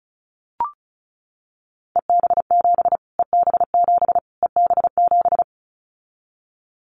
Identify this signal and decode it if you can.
Morse code, heard as a keyed tone, about 34 wpm, 700 Hz. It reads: EB7 EB7 EB7
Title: Callsign Prefixes for Top 25 DXCC - Repeated 3x - 35wpm